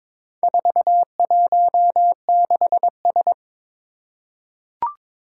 Morse code, heard as a keyed tone, about 22 wpm, 700 Hz. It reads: 416S
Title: Four Challenging Letter-Number Combinations 22wpm